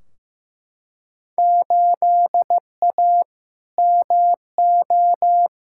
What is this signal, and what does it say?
Morse code, heard as a keyed tone, about 15 wpm, 700 Hz. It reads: 8A MO
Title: Field Day Exchanges Encore - 15wpm